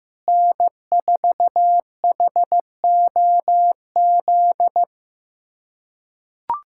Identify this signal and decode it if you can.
Morse code, heard as a keyed tone, about 15 wpm, 700 Hz. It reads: N4HOZ
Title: USA Callsign DE Callsign - 5 Characters - 15wpm